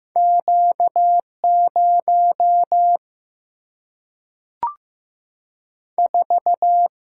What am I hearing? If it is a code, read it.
Morse code, heard as a keyed tone, about 15 wpm, 700 Hz. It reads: Q0 4
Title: Single-Letter Single-Number 15wpm